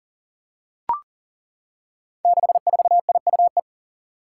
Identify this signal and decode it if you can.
Morse code, heard as a keyed tone, about 39 wpm, 700 Hz. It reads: 64IUE
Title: Five Challenging Letter-Number Combinations 40wpm